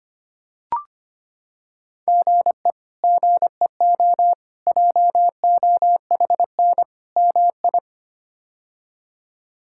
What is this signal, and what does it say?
Morse code, heard as a keyed tone, about 25 wpm, 700 Hz. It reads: GE GEO JOHN MI
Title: K1USN SST QSO Practice - 25wpm